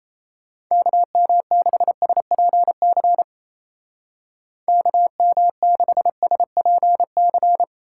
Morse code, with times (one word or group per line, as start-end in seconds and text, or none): 0.7-3.2 KM6SPC
4.7-7.6 KM6SPC